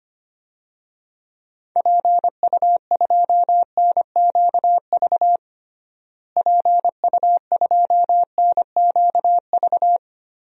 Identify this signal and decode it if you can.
Morse code, heard as a keyed tone, about 25 wpm, 700 Hz. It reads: PU2NQV PU2NQV
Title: DXCC-100 Call Signs - 6 Characters - Repeated 3x - 25wpm